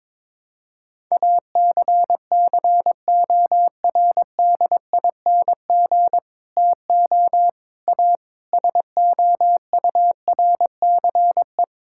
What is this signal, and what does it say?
Morse code, heard as a keyed tone, about 22 wpm, 700 Hz. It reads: ACCORDING TO A SOURCE
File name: Sets of 9 Words - 22wpm